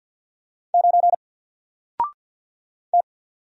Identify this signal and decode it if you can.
Morse code, heard as a keyed tone, about 50 wpm, 700 Hz. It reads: 9 T
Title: Single Letter-Number Alt 50wpm